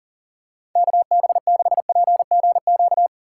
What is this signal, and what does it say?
Morse code, heard as a keyed tone, about 40 wpm, 700 Hz. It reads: KB6PGQ
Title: USA Call Signs - 6 Characters - 2x Character Spacing - 40wpm